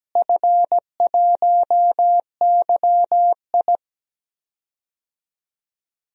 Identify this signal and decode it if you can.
Morse code, heard as a keyed tone, about 17 wpm, 700 Hz. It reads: F1YI
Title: DXCC-100 Call Signs - 4 Characters - Repeated 3x - 17wpm